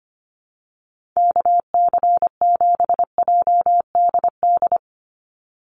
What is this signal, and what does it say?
Morse code, heard as a keyed tone, about 25 wpm, 700 Hz. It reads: KC7JDD
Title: USA Call Signs 25wpm